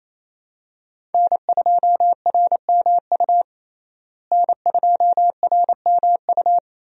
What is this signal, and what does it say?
Morse code, heard as a keyed tone, about 28 wpm, 700 Hz. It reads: N2RMU N2RMU
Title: USA Call Signs - 5 Characters - Repeated 2x - 28wpm